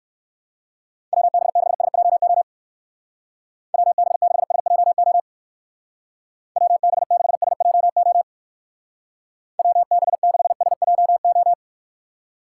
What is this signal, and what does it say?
Morse code, heard as a keyed tone, about 56 wpm, 700 Hz. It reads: WB6SJO WB6SJO WB6SJO WB6SJO
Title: Speed Racing - USA Call Signs 45wpm